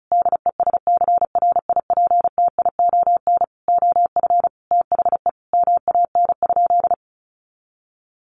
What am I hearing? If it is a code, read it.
Morse code, heard as a keyed tone, about 34 wpm, 700 Hz. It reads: DESCRIPTION OF THE MAN?
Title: Sentences from Top 1500 Words 35wpm